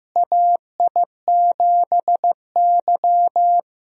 Morse code, heard as a keyed tone, about 16 wpm, 700 Hz. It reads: AI7Y
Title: USA Call Signs - 4 Characters - Repeated 3x - 15wpm